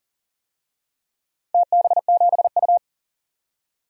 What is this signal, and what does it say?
Morse code, heard as a keyed tone, about 38 wpm, 700 Hz. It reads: TB7U
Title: DXCC-100 Call Signs - 4 Characters - 40wpm